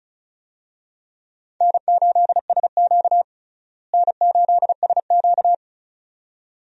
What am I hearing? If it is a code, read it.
Morse code, heard as a keyed tone, about 36 wpm, 700 Hz. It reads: N8SQ N8SQ
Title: USA Call Signs - 4 Characters - Repeated 2x - 35wpm